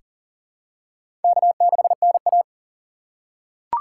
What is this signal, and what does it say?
Morse code, heard as a keyed tone, about 40 wpm, 700 Hz. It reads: K6NA
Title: K1USN SST QSO Practice - 40wpm